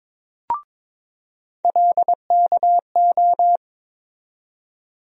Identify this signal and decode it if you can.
Morse code, heard as a keyed tone, about 22 wpm, 700 Hz. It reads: LKO
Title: USA Call Sign Postfixes - 3 Letters 22wpm